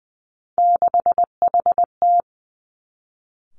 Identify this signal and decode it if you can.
Morse code, heard as a keyed tone, about 20 wpm, 700 Hz. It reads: 6HT
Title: Three Challenging Letter-Number Combinations 20wpm